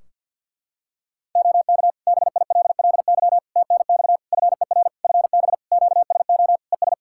Morse code, heard as a keyed tone, about 50 wpm, 700 Hz. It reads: OK BILLY TNX FER FB QSO ES
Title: Practice QSO - Exchange 4 - Ending - 50wpm